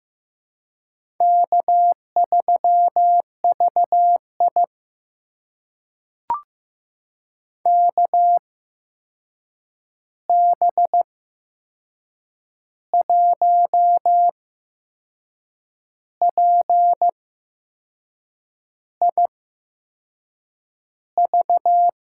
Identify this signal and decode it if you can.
Morse code, heard as a keyed tone, about 15 wpm, spaced as for 6 wpm, 700 Hz. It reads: K3VI KB1PIV
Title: USA Call Signs - 8x Character Spacing - 15wpm